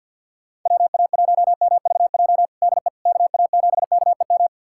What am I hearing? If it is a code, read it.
Morse code, heard as a keyed tone, about 49 wpm, 690 Hz. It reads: WA1MVJ DE KA7KEM
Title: USA Callsign DE Callsign - 6 Characters - 50wpm